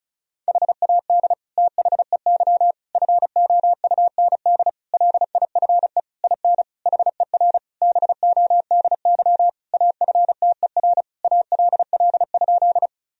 Morse code, heard as a keyed tone, about 36 wpm, 700 Hz. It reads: HAD THEY FOUND LIFE IN HER BODY AFTER ALL?
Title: Sentences from Top 1000 Words - Review Part 2 - 35wpm